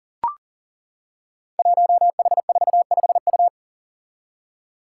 Morse code, heard as a keyed tone, about 41 wpm, 700 Hz. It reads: 1H45U
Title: Five Challenging Letter-Number Combinations 40wpm